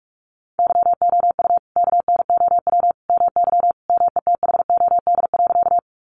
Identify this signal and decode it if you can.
Morse code, heard as a keyed tone, about 44 wpm, 700 Hz. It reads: YOU KNOW MY METHOD.
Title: Sentences from Top 600 Words 45wpm